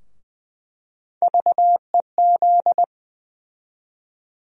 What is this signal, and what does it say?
Morse code, heard as a keyed tone, about 20 wpm, 700 Hz. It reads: VEZ
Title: USA Call Sign Postfixes - 3 Letters 20wpm